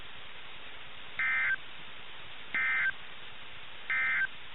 Emergency Alert System (SAME) tones, heard as harsh buzzing bursts. It is heard from 1.1 to 4.3 s.
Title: NOAA Weather Radio - KIH61